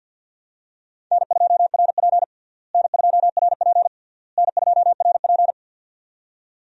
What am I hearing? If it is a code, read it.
Morse code, heard as a keyed tone, about 52 wpm, 700 Hz. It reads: N2RP N2RP N2RP
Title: USA Call Signs - 4 Characters - Repeated 3x - 50wpm